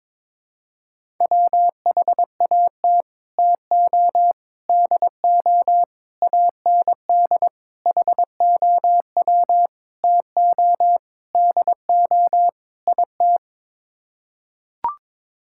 Morse code, heard as a keyed tone, about 21 wpm, 700 Hz. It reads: WHAT TO DO AND HOW TO DO IT
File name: practice / 22wpm / Sets of 8 Words - 22wpm